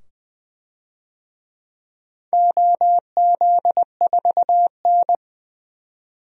Morse code, heard as a keyed tone, about 21 wpm, 700 Hz. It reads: OZ4N
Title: DXCC-100 Call Signs - 4 Characters - 20wpm